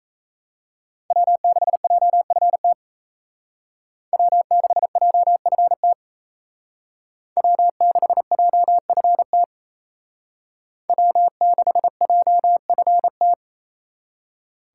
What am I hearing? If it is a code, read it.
Morse code, heard as a keyed tone, about 34 wpm, 700 Hz. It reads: W6JFT W6JFT W6JFT W6JFT
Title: Speed Racing - USA 1x3 Call Signs 28wpm